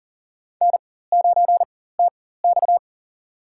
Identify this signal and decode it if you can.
Morse code, heard as a keyed tone, about 40 wpm, spaced as for 19 wpm, 700 Hz. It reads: N9TX
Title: USA Call Signs - 4x Character Spacing - 40wpm